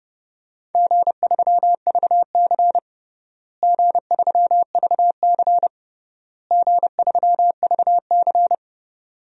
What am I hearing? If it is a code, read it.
Morse code, heard as a keyed tone, about 32 wpm, 700 Hz. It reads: G3VC G3VC G3VC
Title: DXCC-100 Call Signs - 4 Characters - Repeated 3x - 30wpm